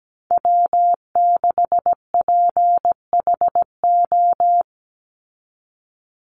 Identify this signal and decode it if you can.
Morse code, heard as a keyed tone, about 17 wpm, 700 Hz. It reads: W6PHO
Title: Call Signs Repeated 3x - 17wpm